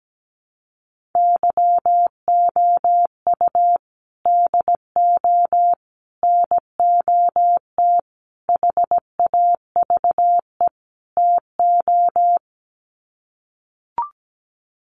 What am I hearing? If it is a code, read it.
Morse code, heard as a keyed tone, about 17 wpm, 700 Hz. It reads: YOU DO NOT HAVE TO
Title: Sets of 5 Words 17wpm